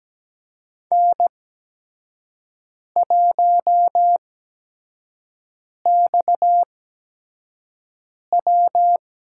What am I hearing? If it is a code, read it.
Morse code, heard as a keyed tone, about 17 wpm, spaced as for 5 wpm, 700 Hz. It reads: N1XW
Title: USA Call Signs - 4 Characters - 8x Character Spacing - 17wpm